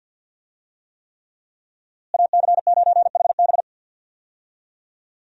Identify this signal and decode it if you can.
Morse code, heard as a keyed tone, about 52 wpm, 700 Hz. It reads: AC9HB
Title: USA Call Signs - 5 Characters - Repeated 3x - 50wpm